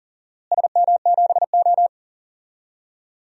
Morse code, heard as a keyed tone, about 37 wpm, 700 Hz. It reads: SM7O
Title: DXCC-100 Call Signs - 4 Characters - Repeated 3x - 40wpm